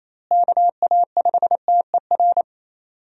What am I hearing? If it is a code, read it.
Morse code, heard as a keyed tone, about 28 wpm, 700 Hz. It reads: KA5TER
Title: USA Call Signs - 6 Characters - Repeated 3x - 28wpm